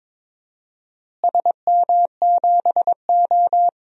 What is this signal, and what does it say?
Morse code, heard as a keyed tone, about 21 wpm, 700 Hz. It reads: SM7O
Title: DXCC-100 Call Signs - 4 Characters - Repeated 3x - 22wpm